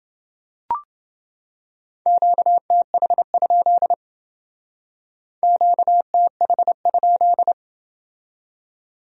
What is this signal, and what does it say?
Morse code, heard as a keyed tone, about 29 wpm, 700 Hz. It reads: QTH? QTH?